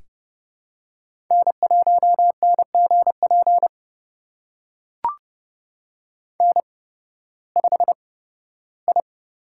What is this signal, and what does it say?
Morse code, heard as a keyed tone, about 31 wpm, spaced as for 16 wpm, 700 Hz. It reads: N1NGP N5I
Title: USA Call Signs - 5 Characters - 8x Character Spacing - 30wpm